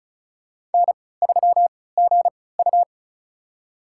Morse code, heard as a keyed tone, about 34 wpm, spaced as for 20 wpm, 700 Hz. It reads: N3GU
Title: USA Call Signs - 4 Characters - 3x Character Spacing - 35wpm